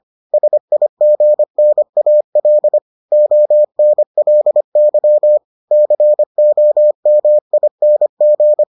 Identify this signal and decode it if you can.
Morse code, heard as a keyed tone, about 25 wpm, 600 Hz. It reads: SIGNAL ONLY COMING